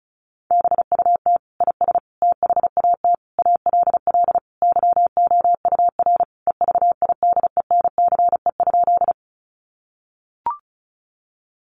Morse code, heard as a keyed tone, about 34 wpm, 700 Hz. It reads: BUT IS THAT ALL YOUR EVIDENCE?
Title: Sentences from Top 1300 Words 35wpm